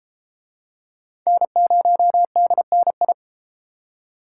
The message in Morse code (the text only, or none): N0DNI